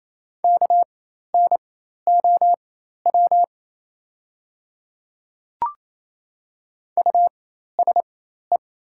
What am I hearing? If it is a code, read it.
Morse code, heard as a keyed tone, about 28 wpm, spaced as for 13 wpm, 690 Hz. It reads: KNOW USE
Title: 4x Character Spacing - Top 100 Words - 28wpm